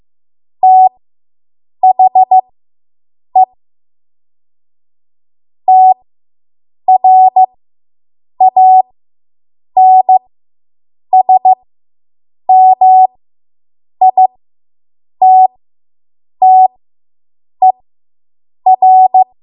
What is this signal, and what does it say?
Morse code, heard as a keyed tone, about 15 wpm, spaced as for 7 wpm, 750 Hz. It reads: THE TRANSMITTER